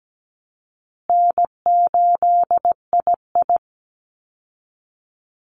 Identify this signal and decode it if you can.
Morse code, heard as a keyed tone, about 17 wpm, 700 Hz. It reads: N8II
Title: Easy Call Signs Repeated 2x - 17wpm